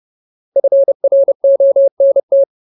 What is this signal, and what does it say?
Morse code, heard as a keyed tone, about 30 wpm, 550 Hz. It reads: FRONT
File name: Ultimate Word List / 30WPM Ultimate